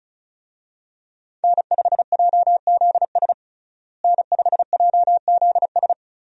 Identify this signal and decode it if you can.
Morse code, heard as a keyed tone, about 36 wpm, 700 Hz. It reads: N5JZS N5JZS
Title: USA Call Signs - 5 Characters - Repeated 2x - 35wpm